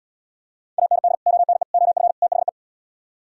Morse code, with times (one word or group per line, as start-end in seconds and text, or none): 0.8-1.1 AND
1.3-1.6 ONE
1.7-2.1 OF
2.2-2.5 THE